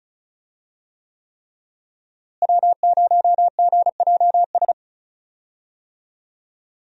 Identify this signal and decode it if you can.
Morse code, heard as a keyed tone, about 36 wpm, 700 Hz. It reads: W0GJS
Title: USA Call Signs - 5 Characters - Repeated 3x - 35wpm